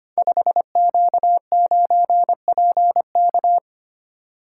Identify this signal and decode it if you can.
Morse code, heard as a keyed tone, about 25 wpm, 700 Hz. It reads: 5Q9PK
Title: Code Groups - 5 Letter:Numbers - 25wpm